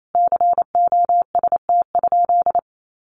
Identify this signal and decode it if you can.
Morse code, heard as a keyed tone, about 28 wpm, 700 Hz. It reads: COST?
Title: Sentences from Top 800 Words 28wpm